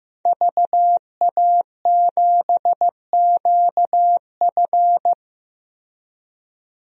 Morse code, heard as a keyed tone, about 15 wpm, 700 Hz. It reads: VA7QF
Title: DXCC-100 Call Signs - 5 Characters - Repeated 3x - 15wpm